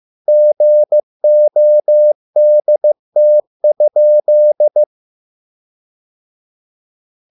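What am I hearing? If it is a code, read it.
Morse code, heard as a keyed tone, about 15 wpm, 600 Hz. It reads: GODT?
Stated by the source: Morse code at a slow 15 wpm from CW med Gnister.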